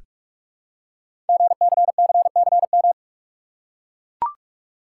Morse code, heard as a keyed tone, about 44 wpm, 700 Hz. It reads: GCCCM
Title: Code Groups - 5 Letters - 45wpm